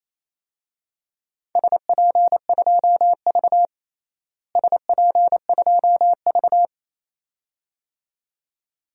Morse code, heard as a keyed tone, about 28 wpm, 700 Hz. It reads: SP2V SP2V
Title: DXCC-100 Call Signs - 4 Characters - Repeated 2x - 28wpm